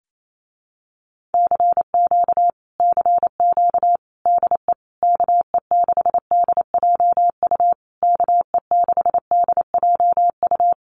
Morse code, heard as a keyed tone, about 28 wpm, 700 Hz. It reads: CQ CQ DE KE6DJU KE6DJU
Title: Common QSO Phrases Encore 28wpm